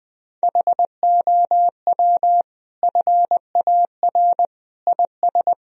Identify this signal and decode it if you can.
Morse code, heard as a keyed tone, about 20 wpm, 700 Hz. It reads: HOW FAR IS
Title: Sentences from Top 1000 Words - Review Part 4 - 20wpm